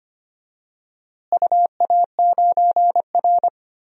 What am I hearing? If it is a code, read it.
Morse code, heard as a keyed tone, about 25 wpm, 700 Hz. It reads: UA9R
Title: DXCC-100 Call Signs - 4 Characters - Repeated 3x - 25wpm